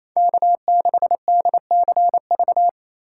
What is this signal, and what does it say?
Morse code, heard as a keyed tone, about 27 wpm, 700 Hz. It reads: K6DCV
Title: USA Call Signs - 3x Character Spacing - 28wpm